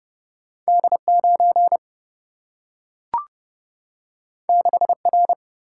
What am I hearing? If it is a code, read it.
Morse code, heard as a keyed tone, about 31 wpm, 700 Hz. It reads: D9 6R
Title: Single-Letter Single-Number 30wpm